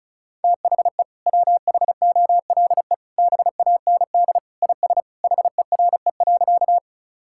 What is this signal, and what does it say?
Morse code, heard as a keyed tone, about 36 wpm, 700 Hz. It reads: THE WHOLE BAND IS HERE.
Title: Sentences from Top 1600 Words 35wpm